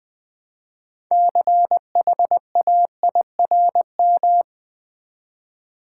Morse code, heard as a keyed tone, about 20 wpm, 700 Hz. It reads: CHAIRM
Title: Build A Word - 2 Letters At A Time - 8-Letter Words - 20wpm